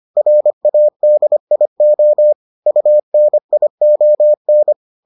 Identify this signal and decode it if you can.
Morse code, heard as a keyed tone, about 25 wpm, 600 Hz. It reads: RADIO UNION